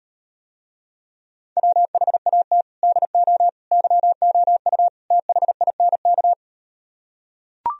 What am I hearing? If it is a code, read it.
Morse code, heard as a keyed tone, about 38 wpm, 700 Hz. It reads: WHAT DO YOU THINK